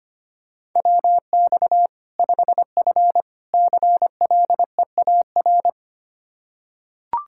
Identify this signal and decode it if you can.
Morse code, heard as a keyed tone, about 25 wpm, 700 Hz. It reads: WX 5F CLEAR
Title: Practice QSO - Exchange 2 - Info - 25wpm